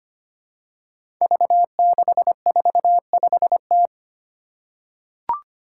Morse code, heard as a keyed tone, about 25 wpm, 700 Hz. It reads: V645T
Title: Five Challenging Letter-Number Combinations 25wpm